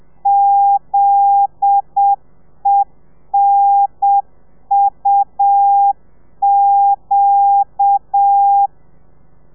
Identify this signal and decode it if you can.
Morse code, heard as a keyed tone, about 7 wpm, 800 Hz. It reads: ZENUQ